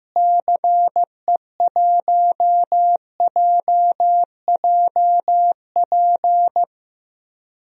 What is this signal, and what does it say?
Morse code, heard as a keyed tone, about 15 wpm, 700 Hz. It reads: CE1JJP
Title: DXCC-100 Call Signs - 6 Characters - Repeated 2x - 15wpm